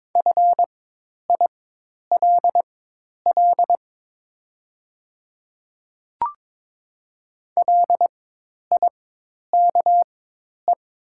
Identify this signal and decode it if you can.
Morse code, heard as a keyed tone, about 22 wpm, spaced as for 10 wpm, 700 Hz. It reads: FILL LIKE
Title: 4x Character Spacing - Top 500 Words - 22wpm